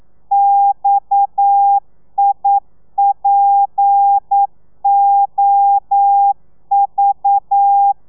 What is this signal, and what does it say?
Morse code, heard as a keyed tone, about 9 wpm, 800 Hz. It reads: XIPOV
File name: telegrafia / i2tfj / Lez7 40